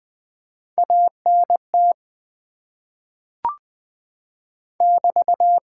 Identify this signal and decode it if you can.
Morse code, heard as a keyed tone, about 20 wpm, 700 Hz. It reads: ANT <BT>
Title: Top 90 QSO Elements - 8x Character Spacing - 20wpm